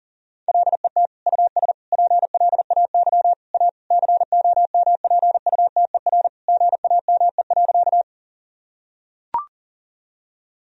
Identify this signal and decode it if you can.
Morse code, heard as a keyed tone, about 40 wpm, 700 Hz. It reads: LET US PLAY A COMPUTER GAME.
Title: Sentences from Top 1000 Words - Review Part 3 - 40wpm